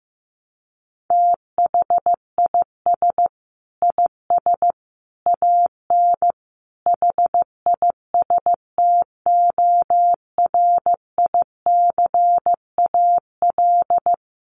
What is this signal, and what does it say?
Morse code, heard as a keyed tone, about 15 wpm, 700 Hz. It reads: THIS IS AN HISTORICAL
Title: Sentences from Top 1300 Words 15wpm